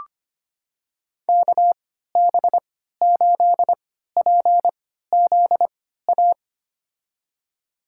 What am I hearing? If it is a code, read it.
Morse code, heard as a keyed tone, about 25 wpm, spaced as for 14 wpm, 700 Hz. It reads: KB8PZA
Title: USA Call Signs - 6 Characters - 3x Character Spacing - 25wpm